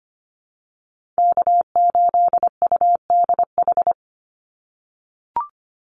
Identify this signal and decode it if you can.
Morse code, heard as a keyed tone, about 25 wpm, 700 Hz. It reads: K8UDH
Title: Call Signs - 25wpm